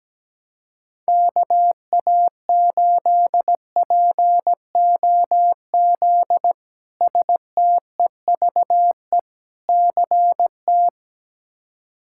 Morse code, heard as a keyed tone, about 17 wpm, 700 Hz. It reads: KA8POZ STEVE CT
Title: K1USN SST QSO Practice - 17wpm